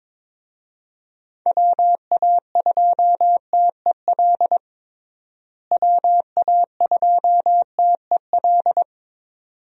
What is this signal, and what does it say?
Morse code, heard as a keyed tone, about 22 wpm, 700 Hz. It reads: WA2TEL WA2TEL
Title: USA Call Signs - 6 Characters - Repeated 2x - 22wpm